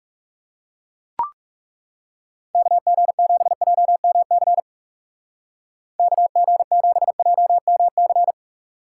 Morse code, heard as a keyed tone, about 40 wpm, 700 Hz. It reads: KG7JMC KG7JMC